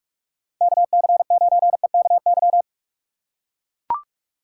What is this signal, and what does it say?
Morse code, heard as a keyed tone, about 44 wpm, 700 Hz. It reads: KC9EKY
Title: USA Call Signs - 6 Characters - 4x Character Spacing - 45wpm